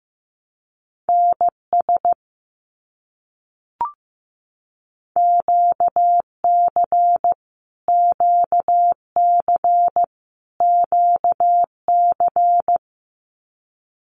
Morse code, heard as a keyed tone, about 15 wpm, 700 Hz. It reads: NS QC QC QC
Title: Canadian Provinces and Territories - Repeated 3x - 15wpm